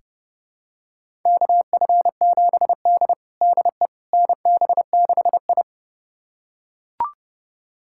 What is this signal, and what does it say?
Morse code, heard as a keyed tone, about 30 wpm, 700 Hz. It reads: KF7D DE NB6I
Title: USA Callsign DE Callsign - 4 Characters - 30wpm